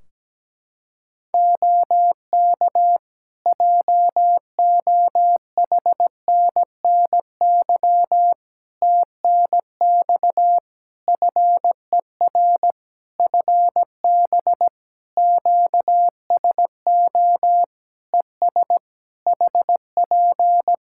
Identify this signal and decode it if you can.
Morse code, heard as a keyed tone, about 17 wpm, 700 Hz. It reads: OK JOHNNY TNX FER FB QSO ES HP
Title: Practice QSO - Exchange 4 - Ending - 17wpm